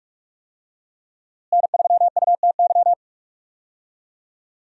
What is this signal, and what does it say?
Morse code, heard as a keyed tone, about 42 wpm, 700 Hz. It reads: N3UTY
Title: USA Call Signs - 5 Characters - Repeated 2x - 45wpm